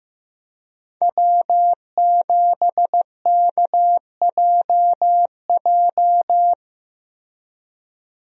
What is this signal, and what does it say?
Morse code, heard as a keyed tone, about 16 wpm, 700 Hz. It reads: W7KJJ
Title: USA Call Signs - 5 Characters - Repeated 2x - 15wpm